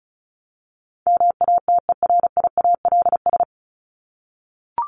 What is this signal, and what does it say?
Morse code, heard as a keyed tone, about 34 wpm, 700 Hz. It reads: MATERIALS
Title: ES:S Suffix - Top 100 Words with Base Words 35wpm